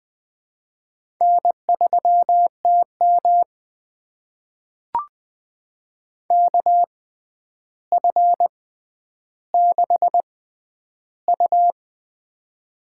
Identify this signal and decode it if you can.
Morse code, heard as a keyed tone, about 20 wpm, spaced as for 10 wpm, 700 Hz. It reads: N3TM KF6U
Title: USA Call Signs - 4 Characters - 6x Character Spacing - 20wpm